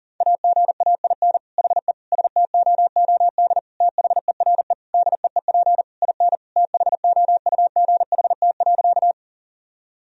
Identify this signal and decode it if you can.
Morse code, heard as a keyed tone, about 40 wpm, 700 Hz. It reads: AGAIN HE STOOD THERE DEEP IN THOUGHT.
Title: Sentences from Top 1000 Words - Review Part 3 - 40wpm